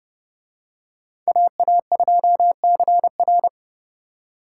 Morse code, heard as a keyed tone, about 29 wpm, 700 Hz. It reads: AA2CR